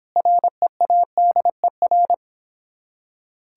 Morse code, heard as a keyed tone, about 26 wpm, 700 Hz. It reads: READER